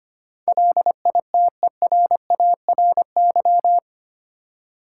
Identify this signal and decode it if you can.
Morse code, heard as a keyed tone, about 25 wpm, 700 Hz. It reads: LITERARY
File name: Eight Letter Words - 25wpm